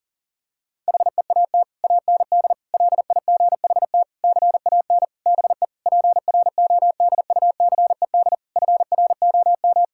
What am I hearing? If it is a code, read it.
Morse code, heard as a keyed tone, about 40 wpm, 700 Hz. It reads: HEAT AND LIGHT CAN BE PRODUCED FROM
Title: Sentences from Top 1000 Words - Review Part 6 - 40wpm